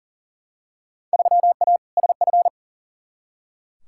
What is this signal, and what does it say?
Morse code, heard as a keyed tone, about 40 wpm, 700 Hz. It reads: 3A SF
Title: Field Day Exchanges Encore - 40wpm